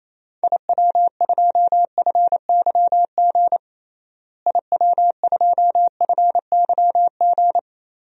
Morse code, heard as a keyed tone, about 28 wpm, 700 Hz. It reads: IW2FYG IW2FYG
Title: DXCC-100 Call Signs - 6 Characters - Repeated 3x - 28wpm